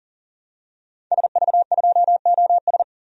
Morse code, heard as a keyed tone, about 40 wpm, 700 Hz. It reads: SV2OS
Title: DXCC-100 Call Signs - 5 Characters - 40wpm